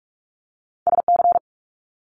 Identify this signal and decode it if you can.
Morse code, heard as a keyed tone, about 44 wpm, 700 Hz. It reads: SC
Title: US States Repeated 3x 45wpm